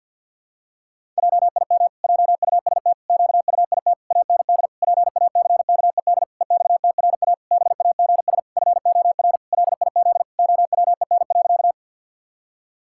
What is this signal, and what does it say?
Morse code, heard as a keyed tone, about 50 wpm, 700 Hz. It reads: JIM JUST QUIT AND PACKED EXTRA BAGS FOR LIZ OWEN.
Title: Pangram Challenge v2 - 50wpm